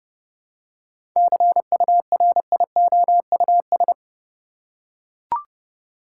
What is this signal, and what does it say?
Morse code, heard as a keyed tone, about 30 wpm, 700 Hz. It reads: CURIOUS
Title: Seven Letter Words - 30wpm